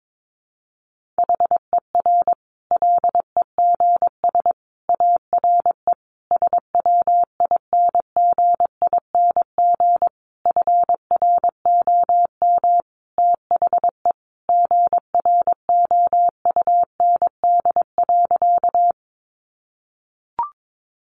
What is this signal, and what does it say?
Morse code, heard as a keyed tone, about 22 wpm, 700 Hz. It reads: HER LEGS ARE SWINGING FROM THE GROUND.
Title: Easy Phrases 22wpm